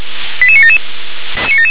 Five-tone selective call (ZVEI), heard as quick runs of five tones, tones of about 70 ms. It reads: 88977
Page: SDR/ZVEI